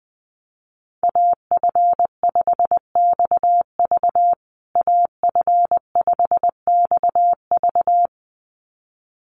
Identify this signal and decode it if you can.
Morse code, heard as a keyed tone, about 20 wpm, 700 Hz. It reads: AF5XV AF5XV
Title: USA Call Signs Repeated 2x 20wpm